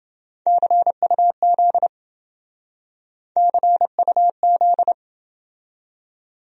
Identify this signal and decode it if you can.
Morse code, heard as a keyed tone, about 28 wpm, 700 Hz. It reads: CUZ CUZ